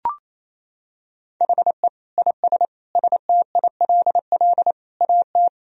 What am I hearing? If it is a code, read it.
Morse code, heard as a keyed tone, about 28 wpm, 700 Hz. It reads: HE IS STILL AT
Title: Sentences from Top 1000 Words - Review Part 1 - 28wpm